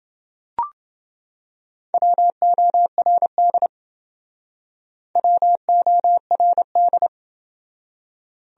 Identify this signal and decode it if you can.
Morse code, heard as a keyed tone, about 29 wpm, 700 Hz. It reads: WORD WORD